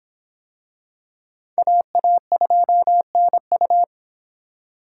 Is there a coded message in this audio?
Yes: Morse code from 1.6 to 3.8 s, heard as a keyed tone, about 25 wpm, 700 Hz. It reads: AA2NU